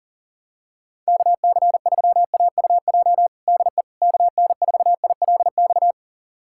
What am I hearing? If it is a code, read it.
Morse code, heard as a keyed tone, about 40 wpm, 700 Hz. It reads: KC3AUJ DE KN4ILX
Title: USA Callsign DE Callsign - 6 Characters - 40wpm